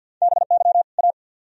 Morse code, heard as a keyed tone, about 66 wpm, 700 Hz. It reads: BY A